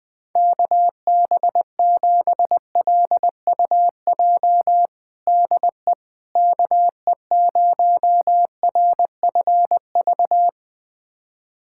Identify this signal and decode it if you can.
Morse code, heard as a keyed tone, about 20 wpm, 700 Hz. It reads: KB7LUJ DE KE0RFV
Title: USA Callsign DE Callsign - 6 Characters - 20wpm